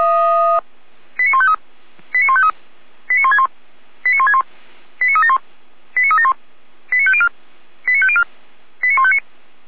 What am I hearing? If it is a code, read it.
Five-tone selective call (ZVEI), heard as quick runs of five tones, tones of about 70 ms. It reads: 89153 89153 89161 89161 89371 89371 89504 89504 89159